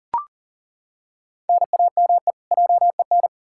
Morse code, heard as a keyed tone, about 40 wpm, 700 Hz. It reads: NAME JEN
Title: K1USN SST QSO Practice 40wpm